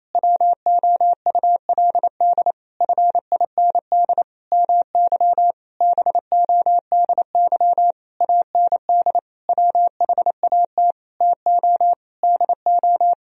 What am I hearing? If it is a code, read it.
Morse code, heard as a keyed tone, about 28 wpm, 700 Hz. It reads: WOULD FIND MY BODY AND WHAT TO DO
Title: Sentences from Top 1000 Words - Review Part 2 - 28wpm